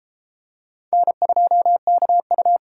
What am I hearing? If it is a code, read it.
Morse code, heard as a keyed tone, about 28 wpm, 700 Hz. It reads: N2KU